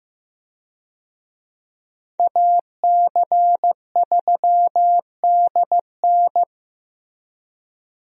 Morse code, heard as a keyed tone, about 15 wpm, 700 Hz. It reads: AC3DN
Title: USA Call Signs - 5 Characters - Repeated 3x - 15wpm